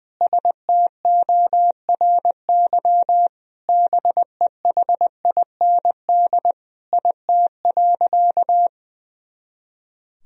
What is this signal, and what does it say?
Morse code, heard as a keyed tone, about 20 wpm, 700 Hz. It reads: STORY BEHIND IT.
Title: Sentences from Top 500 Words and Abbreviations 20wpm